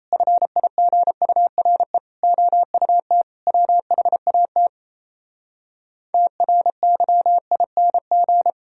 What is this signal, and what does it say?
Morse code, heard as a keyed tone, about 37 wpm, 700 Hz. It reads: FIGURE OUT WHAT TRYING